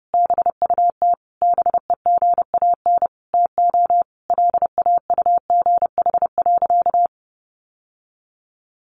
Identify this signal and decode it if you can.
Morse code, heard as a keyed tone, about 30 wpm, 700 Hz. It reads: BUT BEGAN TO LAUGH.
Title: Sentences from Top 600 Words 30wpm